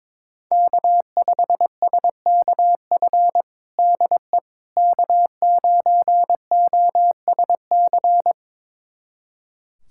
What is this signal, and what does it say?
Morse code, heard as a keyed tone, about 22 wpm, 700 Hz. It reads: K5SKF DE K9OSC
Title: USA Callsign DE Callsign - 5 Characters - 22wpm